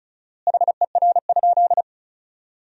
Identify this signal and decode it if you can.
Morse code, heard as a keyed tone, about 36 wpm, 700 Hz. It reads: HER?
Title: 4x Word-Spacing - Sentences from Top 200 Words - 35wpm